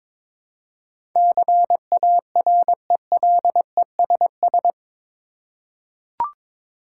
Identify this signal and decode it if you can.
Morse code, heard as a keyed tone, about 22 wpm, 700 Hz. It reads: CARELESS
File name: practice / 22wpm / Eight Letter Words - 22wpm